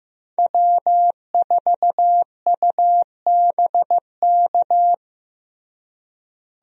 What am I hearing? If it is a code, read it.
Morse code, heard as a keyed tone, about 15 wpm, 700 Hz. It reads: W4UBK
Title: USA Call Signs - 5 Characters - Repeated 3x - 15wpm